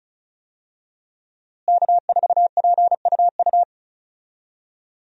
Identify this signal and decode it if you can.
Morse code, heard as a keyed tone, about 36 wpm, 700 Hz. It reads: K4PUU
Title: USA Call Signs - 5 Characters - Repeated 3x - 35wpm